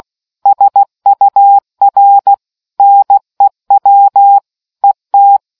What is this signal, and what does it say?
Morse code, heard as a keyed tone, about 16 wpm, 800 Hz. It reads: SUR NEW ET